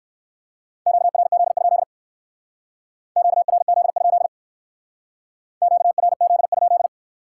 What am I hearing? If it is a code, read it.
Morse code, heard as a keyed tone, about 58 wpm, 700 Hz. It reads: QRZ? QRZ? QRZ?